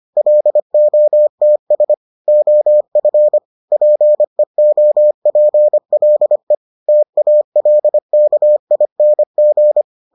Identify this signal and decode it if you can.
Morse code, heard as a keyed tone, about 25 wpm, 600 Hz. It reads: LOTS OF PEOPLE TALKING